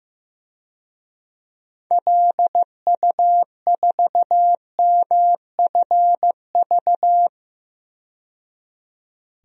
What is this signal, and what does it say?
Morse code, heard as a keyed tone, about 15 wpm, 700 Hz. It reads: LU4MFV
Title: DXCC-100 Call Signs - 6 Characters - 15wpm